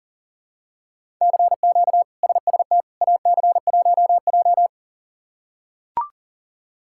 Morse code, heard as a keyed tone, about 40 wpm, 700 Hz. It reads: CQ SST AC1J
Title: K1USN SST QSO Practice - 40wpm